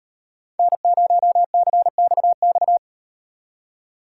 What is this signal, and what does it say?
Morse code, heard as a keyed tone, about 38 wpm, 700 Hz. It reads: N0CXX